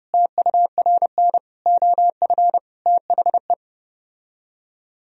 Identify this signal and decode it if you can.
Morse code, heard as a keyed tone, about 31 wpm, 700 Hz. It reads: TURN OF THE